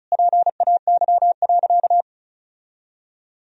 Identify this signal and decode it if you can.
Morse code, heard as a keyed tone, about 36 wpm, 700 Hz. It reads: PAY.
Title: Sentences from Top 1000 Words - Review Part 4 - 35wpm